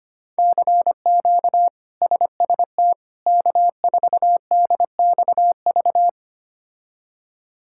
Morse code, heard as a keyed tone, about 25 wpm, 700 Hz. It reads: CQ SST K4DXV
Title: K1USN SST QSO Practice - 25wpm